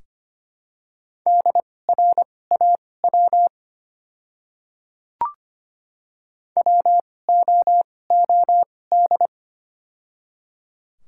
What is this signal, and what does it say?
Morse code, heard as a keyed tone, about 25 wpm, spaced as for 18 wpm, 700 Hz. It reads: DRAW WOOD
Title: 2x Character Spacing - Top 300 Words - 25wpm